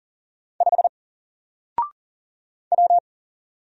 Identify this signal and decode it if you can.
Morse code, heard as a keyed tone, about 40 wpm, 700 Hz. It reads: 5 W
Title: Single Letter-Number Alt 40wpm